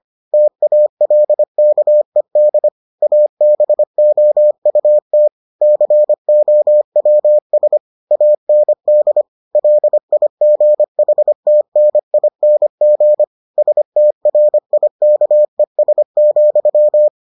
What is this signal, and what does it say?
Morse code, heard as a keyed tone, about 25 wpm, 600 Hz. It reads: TALKED ABOUT COWS AND LIGHTNING STRIKES,